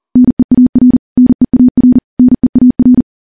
Morse code, heard as a keyed tone, about 40 wpm, 250 Hz. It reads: NEAR NEAR NEAR
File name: morse / morse-training-40WPM